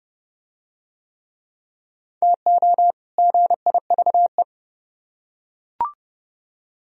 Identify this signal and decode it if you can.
Morse code, heard as a keyed tone, about 26 wpm, 700 Hz. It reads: TO GIVE